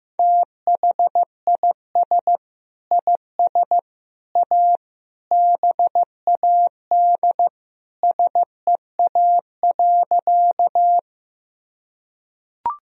Morse code, heard as a keyed tone, about 15 wpm, 700 Hz. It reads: THIS IS A BAD SEA.
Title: Sentences from Top 1000 Words - Review Part 1 - 15wpm